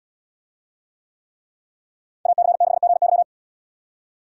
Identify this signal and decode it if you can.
Morse code, heard as a keyed tone, about 75 wpm, 700 Hz. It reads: AC6GY